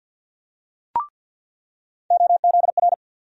Morse code, heard as a keyed tone, about 48 wpm, 700 Hz. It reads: OZR
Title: USA Call Sign Postfixes - 3 Letters 50wpm